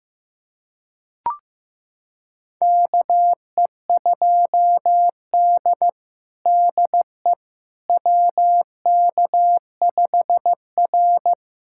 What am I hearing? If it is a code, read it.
Morse code, heard as a keyed tone, about 15 wpm, 700 Hz. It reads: KE2D DE WK5R
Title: USA Callsign DE Callsign - 4 Characters - 15wpm